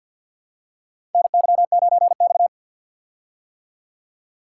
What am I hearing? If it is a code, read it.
Morse code, heard as a keyed tone, about 49 wpm, 700 Hz. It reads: NY9X
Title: K1USN SST QSO Practice - 50wpm